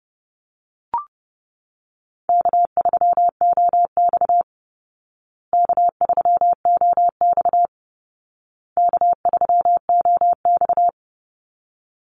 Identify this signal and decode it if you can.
Morse code, heard as a keyed tone, about 30 wpm, 700 Hz. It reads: K3OX K3OX K3OX
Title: Call Signs Repeated 2x - 30wpm